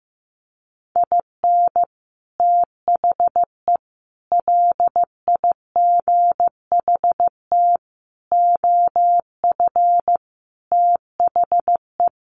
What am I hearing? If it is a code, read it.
Morse code, heard as a keyed tone, about 15 wpm, 700 Hz. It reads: IN THE LIGHT OF THE
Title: Sets of 5 Words Encore2 15wpm